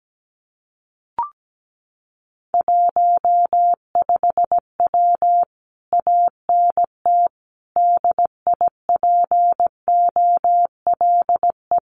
Morse code, heard as a keyed tone, about 17 wpm, 700 Hz. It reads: 15W ANT DIPOLE
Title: Common QSO Phrases Encore 17wpm